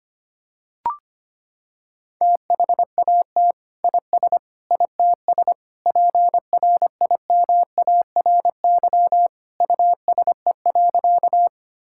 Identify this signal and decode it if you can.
Morse code, heard as a keyed tone, about 25 wpm, 700 Hz. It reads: THAT IS ITS PRIMARY USE.
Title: Sentences from Top 1000 Words - Review Part 5 - 25wpm